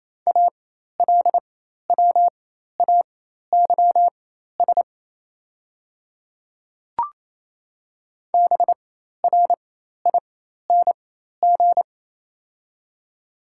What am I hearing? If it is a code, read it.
Morse code, heard as a keyed tone, about 28 wpm, spaced as for 13 wpm, 700 Hz. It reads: ALWAYS BRING
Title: 4x Character Spacing - Top 500 Words - 28wpm